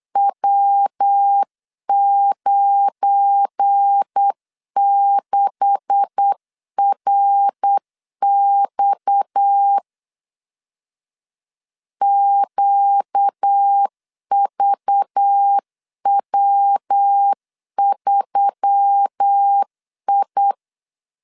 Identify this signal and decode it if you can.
Morse code, heard as a keyed tone, about 8 wpm, 800 Hz. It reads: W96RX QVW3I
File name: LEZIONI / 9 286 8wpm 3425